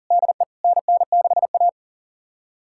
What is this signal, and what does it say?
Morse code, heard as a keyed tone, about 37 wpm, 700 Hz. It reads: DE NN6A
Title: USA Callsign DE Callsign - 4 Characters - 40wpm